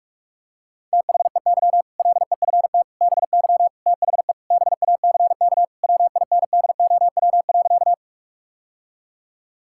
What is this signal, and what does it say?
Morse code, heard as a keyed tone, about 46 wpm, 700 Hz. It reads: THEY LEFT BY THE BACK WINDOW.
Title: Sentences from Top 1600 Words 45wpm